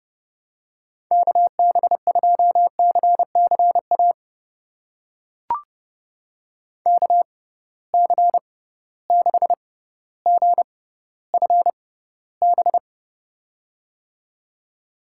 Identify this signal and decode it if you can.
Morse code, heard as a keyed tone, about 29 wpm, spaced as for 15 wpm, 700 Hz. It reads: KB2CCA KC6GFB
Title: USA Call Signs - 6 Characters - 6x Character Spacing - 30wpm